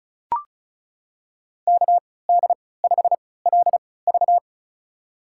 Morse code, heard as a keyed tone, about 36 wpm, spaced as for 20 wpm, 700 Hz. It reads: KD5LV
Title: USA Call Signs - 5 Characters - 3x Character Spacing - 35wpm